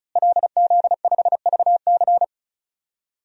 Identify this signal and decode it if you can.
Morse code, heard as a keyed tone, about 36 wpm, 700 Hz. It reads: LZ5VC
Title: DXCC-100 Call Signs - 5 Characters - Repeated 3x - 35wpm